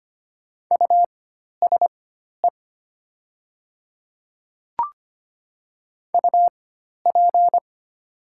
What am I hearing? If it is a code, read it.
Morse code, heard as a keyed tone, about 25 wpm, spaced as for 12 wpm, 700 Hz. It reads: USE UP
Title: 4x Character Spacing - Top 100 Words - 25wpm